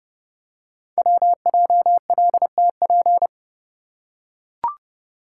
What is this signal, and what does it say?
Morse code, heard as a keyed tone, about 29 wpm, 700 Hz. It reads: WJLTP
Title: Code Groups - 5 Letters - 30wpm